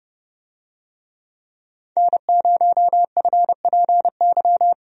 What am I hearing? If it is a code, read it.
Morse code, heard as a keyed tone, about 30 wpm, 700 Hz. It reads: N0FPY
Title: USA Call Signs - 5 Characters - 30wpm